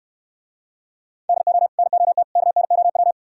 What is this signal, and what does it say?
Morse code, heard as a keyed tone, about 70 wpm, 700 Hz. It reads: DO NOT KNOW